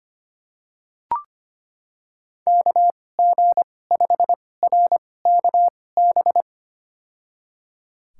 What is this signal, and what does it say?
Morse code, heard as a keyed tone, about 25 wpm, spaced as for 18 wpm, 700 Hz. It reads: KG5RKB
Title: USA Call Signs - 6 Characters - 2x Character Spacing - 25wpm